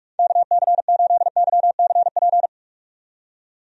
Morse code, heard as a keyed tone, about 46 wpm, 700 Hz. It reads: KC8YCP
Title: USA Call Signs - 6 Characters - 45wpm